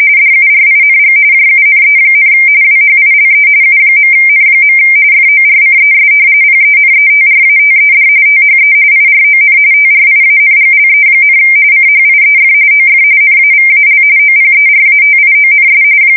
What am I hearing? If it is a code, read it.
RTTY, about 45.45 baud. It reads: DURING JUNE, WHILE U.N. FORCES PENETRATED THE IRON TRIANGLE, CHIEF OF NAVAL OPERATIONS
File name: fsk170 / battleshipnj